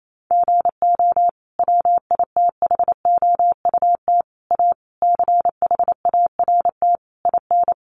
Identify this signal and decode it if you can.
Morse code, heard as a keyed tone, about 28 wpm, 700 Hz. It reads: GO WITHOUT A CHART IN
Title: Sentences from Top 1300 Words 28wpm